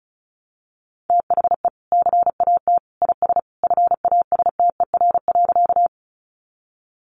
Morse code, heard as a keyed tone, about 34 wpm, 700 Hz. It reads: THE CAT IS FASTER.
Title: Additional Easy Phrases 35wpm